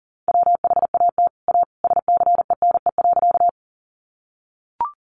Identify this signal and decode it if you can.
Morse code, heard as a keyed tone, about 40 wpm, 700 Hz. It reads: WHAT A SCENE.
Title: Sentences from Top 1100 Words 40wpm